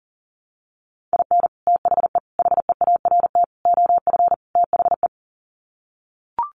Morse code, heard as a keyed tone, about 40 wpm, 700 Hz. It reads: IN THE HEART OF THE
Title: Sets of 5 Words 40wpm